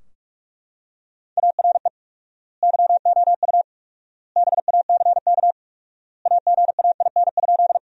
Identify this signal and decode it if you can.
Morse code, heard as a keyed tone, about 43 wpm, 700 Hz. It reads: ARE YOU BACK AGAIN?
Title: 4x Word-Spacing - Sentences from Top 200 Words - 45wpm